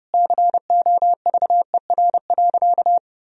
Morse code, heard as a keyed tone, about 32 wpm, 700 Hz. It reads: COVER.
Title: Sentences from Top 1000 Words - Review Part 1 - 30wpm